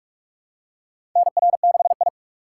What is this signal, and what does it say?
Morse code, heard as a keyed tone, about 44 wpm, 700 Hz. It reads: NR6I
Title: USA Call Signs - 4 Characters - 2x Character Spacing - 45wpm